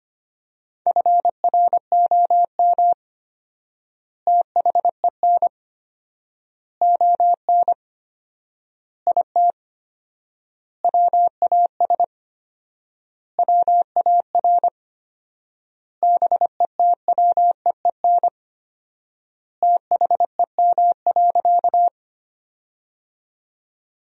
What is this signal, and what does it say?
Morse code, heard as a keyed tone, about 25 wpm, 700 Hz. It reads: FROM THEN ON IT WAS WAR BETWEEN THEM.
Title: 4x Word-Spacing - Sentences from Top 400 Words - 25wpm